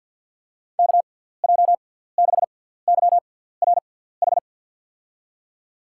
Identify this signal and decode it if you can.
Morse code, heard as a keyed tone, about 47 wpm, spaced as for 17 wpm, 700 Hz. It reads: KJ6YRH
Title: USA Call Signs - 6 Characters - 6x Character Spacing - 50wpm